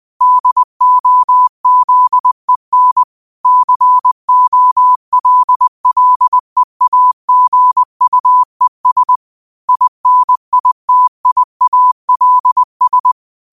Morse code, heard as a keyed tone, about 20 wpm, 1000 Hz. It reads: DOZEN COLLEAGUES INITIALS
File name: qotdfiles / 20wpm